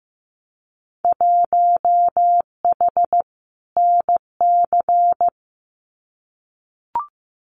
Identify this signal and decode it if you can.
Morse code, heard as a keyed tone, about 15 wpm, 700 Hz. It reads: 1H NC
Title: Winter Field Day Exchanges 15wpm